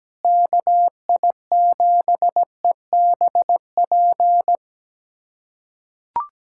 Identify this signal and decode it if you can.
Morse code, heard as a keyed tone, about 17 wpm, 700 Hz. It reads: KI7EBP
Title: USA Callsign DE Callsign - 6 Characters - 17wpm